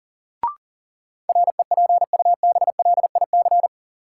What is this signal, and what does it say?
Morse code, heard as a keyed tone, about 40 wpm, 700 Hz. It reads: REPUBLIC
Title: Sentences from Top 1500 Words - New Word Drill - 40wpm